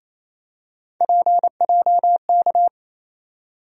Morse code, heard as a keyed tone, about 28 wpm, 700 Hz. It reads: PJK
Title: USA Call Sign Postfixes - 3 Letters 28wpm